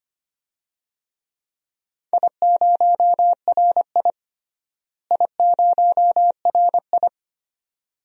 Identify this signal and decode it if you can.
Morse code, heard as a keyed tone, about 25 wpm, 700 Hz. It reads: I0RI I0RI
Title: DXCC-100 Call Signs - 4 Characters - Repeated 2x - 25wpm